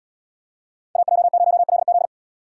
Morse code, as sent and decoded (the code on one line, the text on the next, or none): .- .--- ----- .-.. --..
AJ0LZ